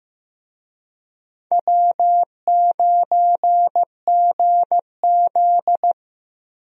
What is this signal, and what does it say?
Morse code, heard as a keyed tone, about 15 wpm, 700 Hz. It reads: W9GZ
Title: USA Call Signs - 4 Characters - Repeated 3x - 15wpm